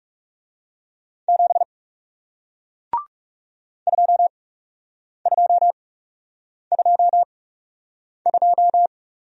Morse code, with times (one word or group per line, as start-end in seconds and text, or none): 1.3-1.6 7
3.9-8.9 2222